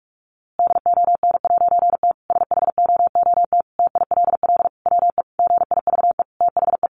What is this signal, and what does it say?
Morse code, heard as a keyed tone, about 44 wpm, 700 Hz. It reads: DON'T SHOOT TILL WE GIVE THE
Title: A Sound Of Thunder 45wpm